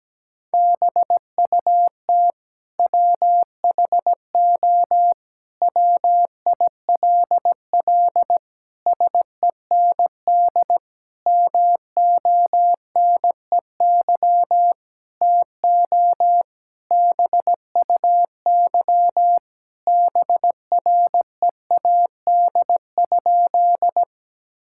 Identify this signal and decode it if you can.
Morse code, heard as a keyed tone, about 17 wpm, 700 Hz. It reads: BUT WHO WILL SEND MONEY TO BUY BREAD?
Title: Sentences from Top 1600 Words 17wpm